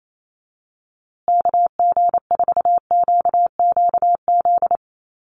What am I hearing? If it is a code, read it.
Morse code, heard as a keyed tone, about 28 wpm, 700 Hz. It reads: KG4QQZ
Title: USA Call Signs Repeated 2x 28wpm